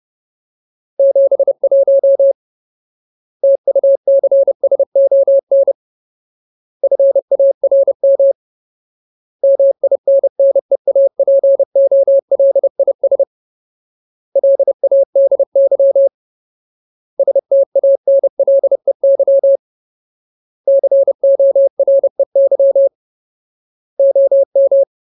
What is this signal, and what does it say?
Morse code, heard as a keyed tone, about 30 wpm, 550 Hz. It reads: 71 TUCSON FARM MINNEAPOLIS LADY STANLEY COREY OM
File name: Ultimate Word List / 30WPM Ultimate